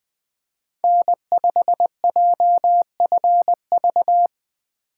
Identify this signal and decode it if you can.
Morse code, heard as a keyed tone, about 20 wpm, 700 Hz. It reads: N5JFV
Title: USA Call Signs - 5 Characters - 4x Character Spacing - 20wpm